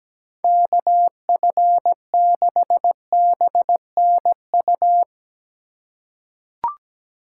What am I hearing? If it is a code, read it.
Morse code, heard as a keyed tone, about 17 wpm, 700 Hz. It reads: KF6BNU
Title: USA Call Signs - 3x Character Spacing - 17wpm